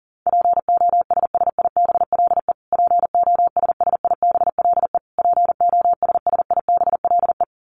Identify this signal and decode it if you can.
Morse code, heard as a keyed tone, about 40 wpm, 700 Hz. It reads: POSSIBLE POSSIBLE POSSIBLE
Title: Top 500 Words Repeated 40wpm